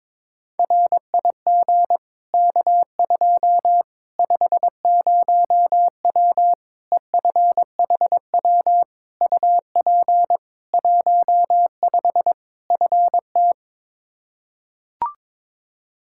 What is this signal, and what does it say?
Morse code, heard as a keyed tone, about 21 wpm, 700 Hz. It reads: RIG K2 50W EFHW UP 15 FT
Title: Practice QSO - Exchange 2 - Info - 22wpm